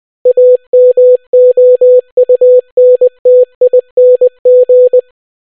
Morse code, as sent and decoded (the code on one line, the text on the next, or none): .- -- --- ..- -. - .. -. --.
AMOUNTING